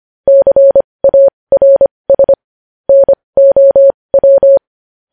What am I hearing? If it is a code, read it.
Morse code, heard as a keyed tone, about 25 wpm, spaced as for 20 wpm, 550 Hz. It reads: CARS NOW
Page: news-headlines-in-morse-code-at-20-wpm